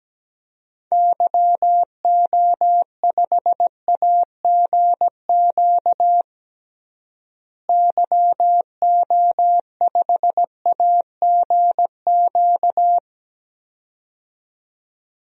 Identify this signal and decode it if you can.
Morse code, heard as a keyed tone, about 17 wpm, 700 Hz. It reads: YO5AGQ YO5AGQ
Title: DXCC-100 Call Signs - 6 Characters - Repeated 2x - 17wpm